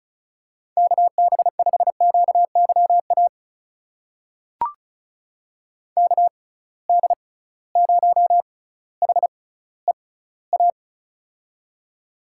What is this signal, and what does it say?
Morse code, heard as a keyed tone, about 36 wpm, spaced as for 18 wpm, 700 Hz. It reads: KB5QYA KD0HEA
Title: USA Call Signs - 6 Characters - 6x Character Spacing - 35wpm